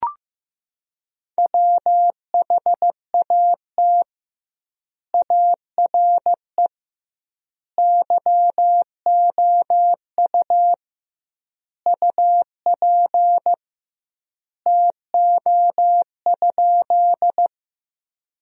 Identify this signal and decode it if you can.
Morse code, heard as a keyed tone, about 15 wpm, 700 Hz. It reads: WHAT ARE YOU UP TO?
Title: 2x Word-Spacing - Sentences from Top 100 Words - 15wpm